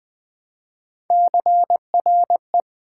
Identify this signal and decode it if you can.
Morse code, heard as a keyed tone, about 20 wpm, 700 Hz. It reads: CRE
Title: Build A Word - 1 Letter At A Time - 5-Letter Words - 20wpm